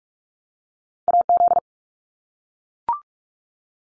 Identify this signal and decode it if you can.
Morse code, heard as a keyed tone, about 44 wpm, 700 Hz. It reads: AZ
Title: ARRL Sections 45wpm